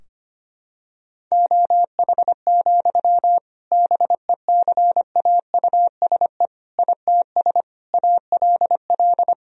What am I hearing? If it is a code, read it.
Morse code, heard as a keyed tone, about 25 wpm, 700 Hz. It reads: OH, BECAUSE ITS ALL
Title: Sentences from Top 1000 Words - Review Part 3 - 25wpm